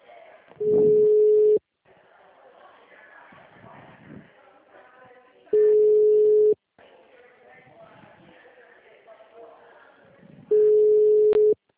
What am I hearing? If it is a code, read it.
Ringback tone (Europe, 425 Hz), with speech.
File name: out- -2001- - - .7611